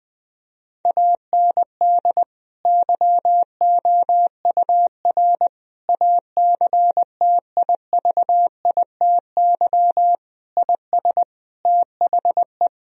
Morse code, heard as a keyed tone, about 21 wpm, 700 Hz. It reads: AND YOUR ACTIVITY IS THE
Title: Sentences from Top 1000 Words - Review Part 4 - 20wpm